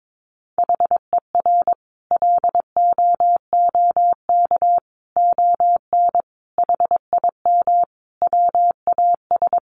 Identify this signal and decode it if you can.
Morse code, heard as a keyed tone, about 22 wpm, 700 Hz. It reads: HER LOOK ON HIM WAS
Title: Sentences from Top 1400 Words 22wpm